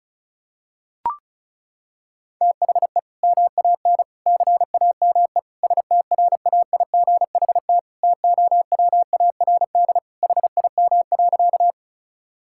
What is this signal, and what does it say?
Morse code, heard as a keyed tone, about 36 wpm, 700 Hz. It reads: THE MAN CAME STRAIGHT TOWARD HIM.
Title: Sentences from Top 1000 Words - Review Part 6 - 35wpm